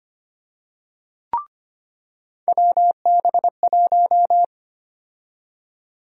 Morse code, heard as a keyed tone, about 25 wpm, 700 Hz. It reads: WB1
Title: USA Call Sign Prefixes - 25wpm